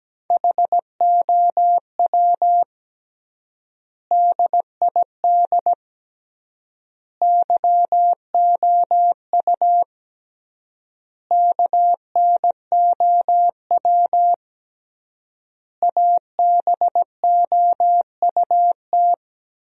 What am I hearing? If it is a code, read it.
Morse code, heard as a keyed tone, about 17 wpm, 700 Hz. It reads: HOW DID YOU KNOW ABOUT
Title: 3x Word-Spacing - Sentences from Top 100 Words - 17wpm